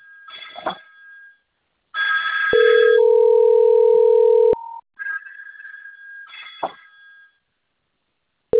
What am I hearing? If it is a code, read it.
Ringback tone (North America), 440 + 480 Hz.
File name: rg-1600-unknown- - - .79348